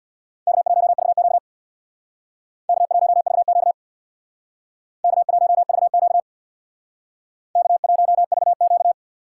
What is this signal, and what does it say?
Morse code, heard as a keyed tone, about 63 wpm, 700 Hz. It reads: K1VQ K1VQ K1VQ K1VQ